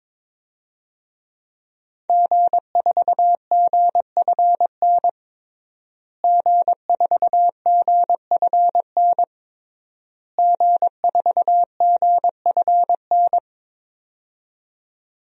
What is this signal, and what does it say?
Morse code, heard as a keyed tone, about 22 wpm, 700 Hz. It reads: G4GFN G4GFN G4GFN
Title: DXCC-100 Call Signs - 5 Characters - Repeated 3x - 22wpm